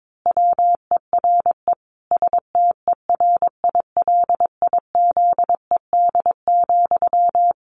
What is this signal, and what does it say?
Morse code, heard as a keyed tone, about 22 wpm, 700 Hz. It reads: WERE STERILIZED,
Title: A Sound Of Thunder 22wpm